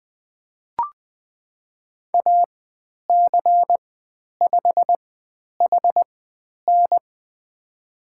Morse code, heard as a keyed tone, about 20 wpm, spaced as for 10 wpm, 700 Hz. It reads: AC5HN
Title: 109 Lesson 18g - Letter m - Call Signs 20wpm